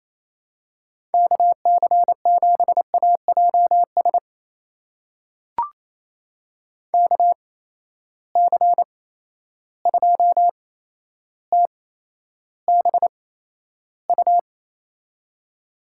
Morse code, heard as a keyed tone, about 28 wpm, spaced as for 12 wpm, 700 Hz. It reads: KC7AJS KC2TBU
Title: USA Call Signs - 6 Characters - 8x Character Spacing - 28wpm